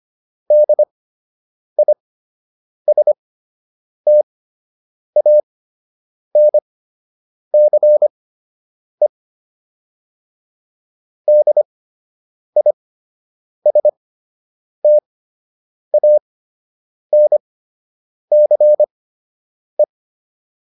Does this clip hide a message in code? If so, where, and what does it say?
Yes: Morse code from 0.5 to 19.8 s, heard as a keyed tone, about 25 wpm, spaced as for 8 wpm, 600 Hz. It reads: DISTANCE DISTANCE